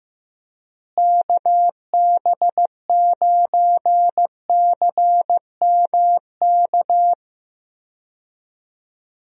Morse code, heard as a keyed tone, about 15 wpm, 700 Hz. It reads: KB9CMK
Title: USA Call Signs - 6 Characters - Repeated 2x - 15wpm